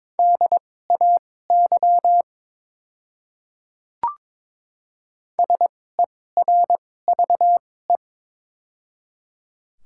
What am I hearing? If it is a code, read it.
Morse code, heard as a keyed tone, about 22 wpm, spaced as for 16 wpm, 700 Hz. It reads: DAY SERVE
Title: 2x Character Spacing - Top 500 Words - 22wpm